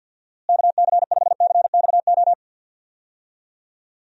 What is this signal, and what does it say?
Morse code, heard as a keyed tone, about 49 wpm, 700 Hz. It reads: KC5CXO
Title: USA Call Signs - 6 Characters - Repeated 2x - 50wpm